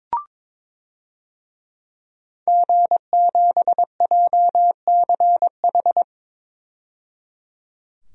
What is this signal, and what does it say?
Morse code, heard as a keyed tone, about 22 wpm, 700 Hz. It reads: G7JCH
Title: DXCC-100 Call Signs - 5 Characters - 22wpm